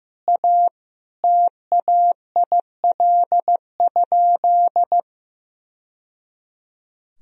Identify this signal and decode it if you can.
Morse code, heard as a keyed tone, about 15 wpm, 700 Hz. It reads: A TAIL?
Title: Sentences from Top 1000 Words - Review Part 3 - 15wpm